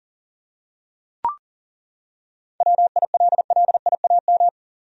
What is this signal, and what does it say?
Morse code, heard as a keyed tone, about 40 wpm, 700 Hz. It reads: WILLIAM
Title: USA Common Names - 40wpm